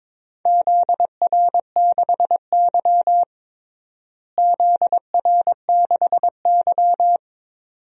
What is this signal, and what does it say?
Morse code, heard as a keyed tone, about 21 wpm, 700 Hz. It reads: ZR6Y ZR6Y
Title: DXCC-100 Call Signs - 4 Characters - Repeated 2x - 22wpm